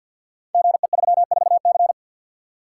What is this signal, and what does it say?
Morse code, heard as a keyed tone, about 49 wpm, 700 Hz. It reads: GE34C
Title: Code Groups - 5 Letter:Numbers - 50wpm